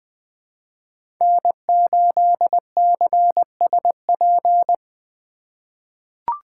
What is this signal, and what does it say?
Morse code, heard as a keyed tone, about 20 wpm, 700 Hz. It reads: N8CSP
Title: USA Call Signs - 3x Character Spacing - 20wpm